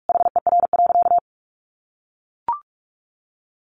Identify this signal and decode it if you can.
Morse code, heard as a keyed tone, about 44 wpm, 700 Hz. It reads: HER.
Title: Sentences from Top 1500 Words 45wpm